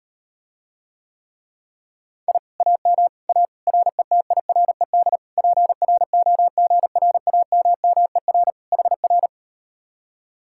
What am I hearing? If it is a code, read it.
Morse code, heard as a keyed tone, about 38 wpm, 700 Hz. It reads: I AM A RETIRED PROGRAMMER HR